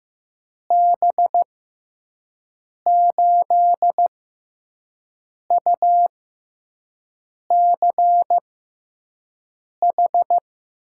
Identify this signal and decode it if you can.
Morse code, heard as a keyed tone, about 15 wpm, spaced as for 5 wpm, 700 Hz. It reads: B8UCH
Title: USA Call Signs - 6 Characters - 6x Character Spacing - 15wpm